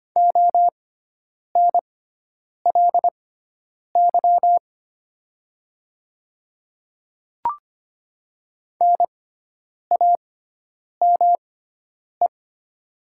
Morse code, heard as a keyed tone, about 25 wpm, spaced as for 9 wpm, 700 Hz. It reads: ONLY NAME
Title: 6x Character Spacing - Top 200 Words - 25wpm